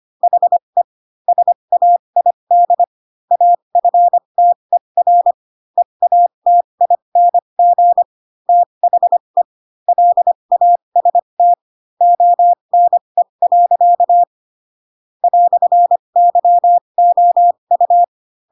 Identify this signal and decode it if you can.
Morse code, heard as a keyed tone, about 25 wpm, spaced as for 22 wpm, 700 Hz. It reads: HE SAID AFTER EATING THE LAST ONE. "YOU
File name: Pinocchio / Pinocchio ch7 18 25 ik1hge